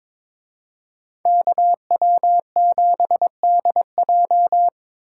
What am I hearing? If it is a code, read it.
Morse code, heard as a keyed tone, about 26 wpm, 700 Hz. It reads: KW7DJ
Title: Speed Racing - USA 2x2 Call Signs 22wpm